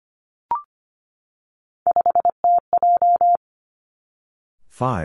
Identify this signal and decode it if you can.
Morse code, heard as a keyed tone, about 25 wpm, 700 Hz. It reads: 5TJ
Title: Three Challenging Letter-Number Combinations 25wpm